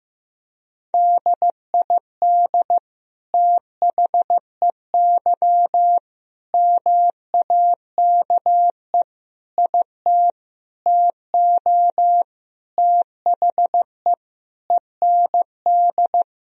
Morse code, heard as a keyed tone, about 15 wpm, 700 Hz. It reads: DID THEY MAKE IT TO THE END
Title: Sentences from Top 1000 Words - Review Part 3 - 15wpm